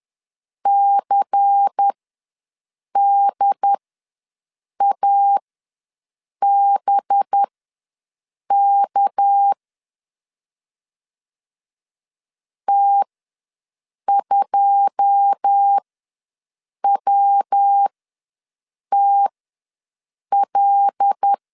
Morse code, heard as a keyed tone, about 10 wpm, spaced as for 6 wpm, 800 Hz. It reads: CDABK T2WTL
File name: LEZIONI / 9 2 ws 10wpm 4130